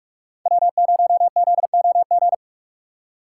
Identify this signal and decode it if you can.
Morse code, heard as a keyed tone, about 45 wpm, 690 Hz. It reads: W0ZOG
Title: USA Call Signs - 5 Characters - 4x Character Spacing - 45wpm